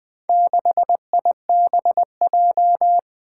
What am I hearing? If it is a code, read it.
Morse code, heard as a keyed tone, about 20 wpm, 700 Hz. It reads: 6IBJ
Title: Four Challenging Letter-Number Combinations 20wpm